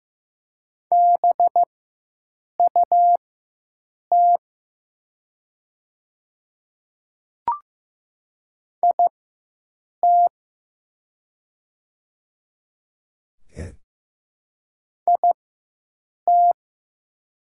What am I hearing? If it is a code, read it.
Morse code, heard as a keyed tone, about 15 wpm, spaced as for 7 wpm, 700 Hz. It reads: BUT IT IT
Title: 4x Character Spacing - Top 100 Words - 15wpm